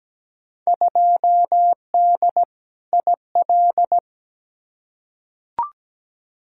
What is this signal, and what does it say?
Morse code, heard as a keyed tone, about 17 wpm, 700 Hz. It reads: 2D IL
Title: Field Day Exchanges - 17wpm